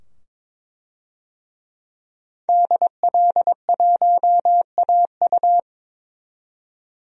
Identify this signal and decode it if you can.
Morse code, heard as a keyed tone, about 22 wpm, 700 Hz. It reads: DL1AU
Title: DXCC-100 Call Signs - 5 Characters - 22wpm